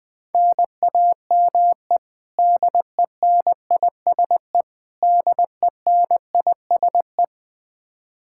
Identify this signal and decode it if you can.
Morse code, heard as a keyed tone, about 20 wpm, 700 Hz. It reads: NAME DENISE DENISE
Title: Practice QSO - Exchange 1 - Beginning - 20wpm